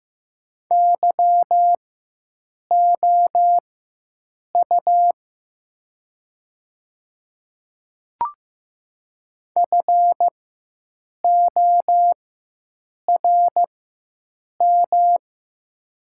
Morse code, heard as a keyed tone, about 15 wpm, spaced as for 7 wpm, 700 Hz. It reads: YOU FORM
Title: 4x Character Spacing - Top 500 Words - 15wpm